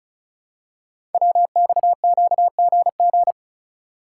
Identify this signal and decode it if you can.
Morse code, heard as a keyed tone, about 34 wpm, 700 Hz. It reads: WXQGG
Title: Code Groups - 5 Letters - 35wpm